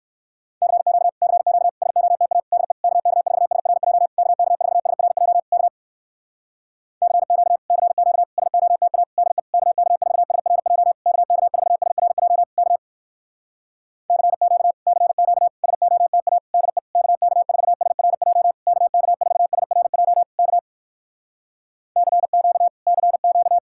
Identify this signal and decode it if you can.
Morse code, heard as a keyed tone, about 69 wpm, 700 Hz. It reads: CQ CQ SOTA DE KK4SRJ KK4SRJ K CQ CQ SOTA DE KK4SRJ KK4SRJ K CQ CQ SOTA DE KK4SRJ KK4SRJ K CQ CQ